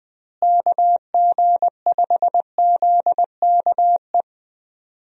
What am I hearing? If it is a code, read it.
Morse code, heard as a keyed tone, about 20 wpm, 700 Hz. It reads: KG5ZKE
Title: USA Call Signs - 6 Characters - Repeated 2x - 20wpm